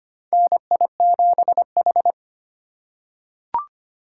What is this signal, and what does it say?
Morse code, heard as a keyed tone, about 25 wpm, 700 Hz. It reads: NI7H
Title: USA Call Signs - 4 Characters - 4x Character Spacing - 25wpm